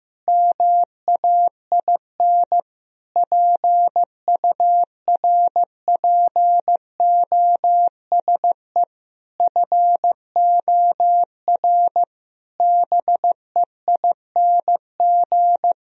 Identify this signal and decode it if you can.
Morse code, heard as a keyed tone, about 15 wpm, 700 Hz. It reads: MAIN PURPOSE FOR BEING
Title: Sentences from Top 1000 Words - Review Part 2 - 15wpm